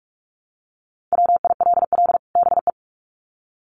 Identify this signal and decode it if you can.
Morse code, heard as a keyed tone, about 44 wpm, 700 Hz. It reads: WILL BE
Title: Sets of 2 Words 45wpm